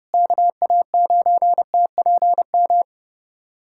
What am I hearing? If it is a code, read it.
Morse code, heard as a keyed tone, about 30 wpm, 700 Hz. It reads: KA9TPM
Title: USA Callsign DE Callsign - 6 Characters - 30wpm